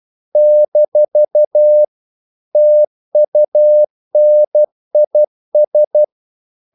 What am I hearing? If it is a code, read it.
Morse code, heard as a keyed tone, about 12 wpm, 600 Hz. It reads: - TUNIS
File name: CW / Practice / NationalCapitals12